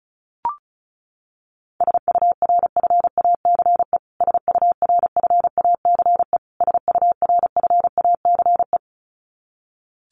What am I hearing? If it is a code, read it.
Morse code, heard as a keyed tone, about 34 wpm, 700 Hz. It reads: SURFACE SURFACE SURFACE
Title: Top 500 Words Repeated 35wpm